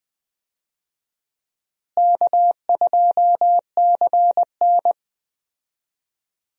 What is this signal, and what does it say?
Morse code, heard as a keyed tone, about 20 wpm, 700 Hz. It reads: K2CN
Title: USA Call Signs - 4 Characters - Repeated 2x - 20wpm